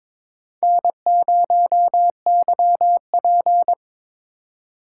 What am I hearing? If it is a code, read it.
Morse code, heard as a keyed tone, about 22 wpm, 700 Hz. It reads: N0YP
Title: USA Call Signs - 4x Character Spacing - 22wpm